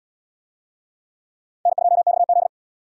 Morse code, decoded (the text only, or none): N2XZ